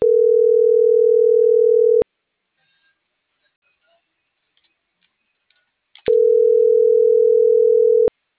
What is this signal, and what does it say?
Ringback tone (North America), 440 + 480 Hz.